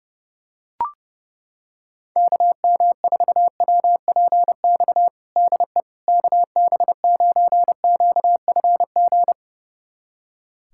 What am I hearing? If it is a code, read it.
Morse code, heard as a keyed tone, about 30 wpm, 700 Hz. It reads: KM4WPX DE KB9QFG
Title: USA Callsign DE Callsign - 6 Characters - 30wpm